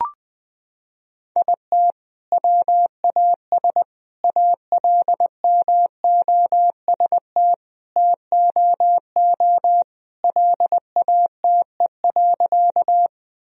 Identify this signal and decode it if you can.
Morse code, heard as a keyed tone, about 20 wpm, 700 Hz. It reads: IT WAS ALMOST TOO LATE.
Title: Sentences from Top 1000 Words - Review Part 4 - 20wpm